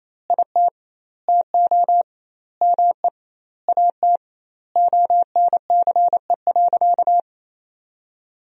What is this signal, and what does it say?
Morse code, heard as a keyed tone, about 28 wpm, 700 Hz. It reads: IT TO ME AT ONCE.
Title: 2x Word-Spacing - Sentences from Top 500 Words - 28wpm